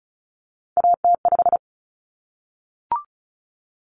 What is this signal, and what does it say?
Morse code, heard as a keyed tone, about 34 wpm, 700 Hz. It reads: AT5
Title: Common Cut Numbers - 3-Digit 35wpm